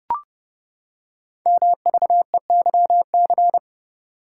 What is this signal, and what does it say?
Morse code, heard as a keyed tone, about 30 wpm, 700 Hz. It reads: MVEYC
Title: Code Groups - 5 Letters - 30wpm